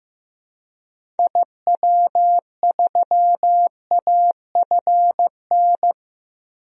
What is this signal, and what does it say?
Morse code, heard as a keyed tone, about 15 wpm, 700 Hz. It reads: IW3AFN
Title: DXCC-100 Call Signs - 6 Characters - 15wpm